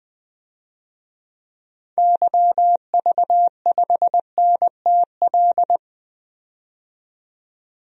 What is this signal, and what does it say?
Morse code, heard as a keyed tone, about 20 wpm, 700 Hz. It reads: YV5NTL
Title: DXCC-100 Call Signs - 6 Characters - 20wpm